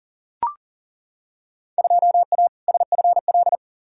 Morse code, heard as a keyed tone, about 40 wpm, 700 Hz. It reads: 2A SFL
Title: Field Day Exchanges Encore - 40wpm